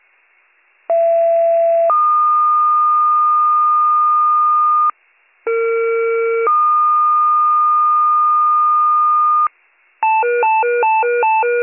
Two-tone page, heard as a one-second tone, then a three-second tone, about 670 then 1220 Hz, about 460 then 1220 Hz. It is heard from 0.9 to 9.5 s.